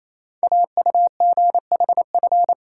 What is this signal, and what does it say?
Morse code, heard as a keyed tone, about 28 wpm, 700 Hz. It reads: AUGHF
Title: Code Groups - 5 Letters - 28wpm